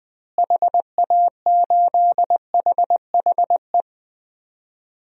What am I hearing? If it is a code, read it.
Morse code, heard as a keyed tone, about 20 wpm, 700 Hz. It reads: HA8HHE
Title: DXCC-100 Call Signs - 6 Characters - Repeated 2x - 20wpm